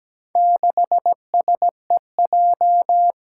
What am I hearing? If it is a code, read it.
Morse code, heard as a keyed tone, about 17 wpm, 700 Hz. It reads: 6SEJ
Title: Four Challenging Letter-Number Combinations 17wpm